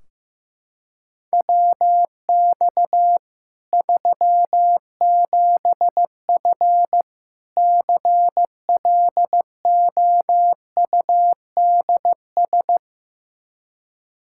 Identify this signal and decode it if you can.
Morse code, heard as a keyed tone, about 15 wpm, 700 Hz. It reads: WX 37F CLOUDS
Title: Practice QSO - Exchange 2 - Info - 15wpm